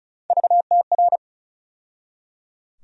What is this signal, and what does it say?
Morse code, heard as a keyed tone, about 35 wpm, 700 Hz. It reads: VTR
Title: USA Call Sign Postfixes - 3 Letters 35wpm